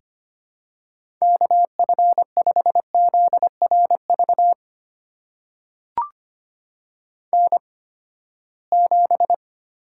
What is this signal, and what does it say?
Morse code, heard as a keyed tone, about 25 wpm, spaced as for 17 wpm, 700 Hz. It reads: KF5ZRV N7
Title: USA Call Signs - 8x Character Spacing - 25wpm